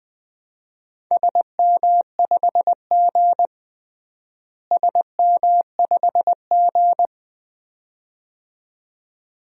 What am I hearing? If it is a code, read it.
Morse code, heard as a keyed tone, about 21 wpm, 700 Hz. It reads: SM5G SM5G
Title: DXCC-100 Call Signs - 4 Characters - Repeated 2x - 20wpm